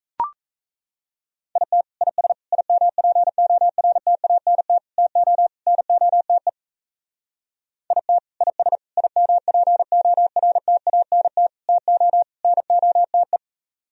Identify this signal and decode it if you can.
Morse code, heard as a keyed tone, about 32 wpm, 700 Hz. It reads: IT IS IMPORTANT TO NOTE IT IS IMPORTANT TO NOTE
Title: Speed-Racing - Sets of 5 Words 28wpm